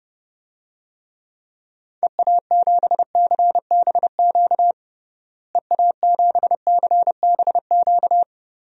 Morse code, heard as a keyed tone, about 30 wpm, 700 Hz. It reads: EA7CBQ EA7CBQ
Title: DXCC-100 Call Signs - 6 Characters - Repeated 3x - 30wpm